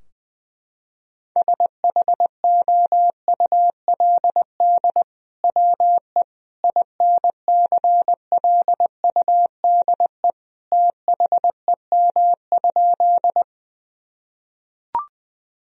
Morse code, heard as a keyed tone, about 21 wpm, 700 Hz. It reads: SHOULD WE INCLUDE THEM?
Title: Sentences from Top 1000 Words - Review Part 3 - 20wpm